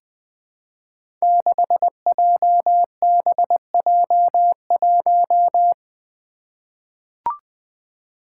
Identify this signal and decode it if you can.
Morse code, heard as a keyed tone, about 20 wpm, 700 Hz. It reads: 6JBJ1
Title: Five Challenging Letter-Number Combinations 20wpm